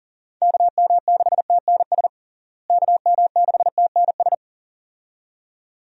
Morse code, heard as a keyed tone, about 40 wpm, 700 Hz. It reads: KM6TNS KM6TNS
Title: USA Call Signs - 6 Characters - Repeated 2x - 40wpm